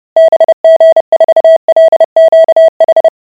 Morse code, heard as a keyed tone, about 30 wpm, 650 Hz. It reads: BG4LQH